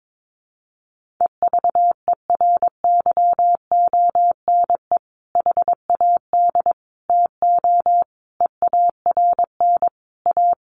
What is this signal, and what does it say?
Morse code, heard as a keyed tone, about 22 wpm, 700 Hz. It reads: EVERYONE HAD TO EARN A
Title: Sentences from Top 1500 Words 22wpm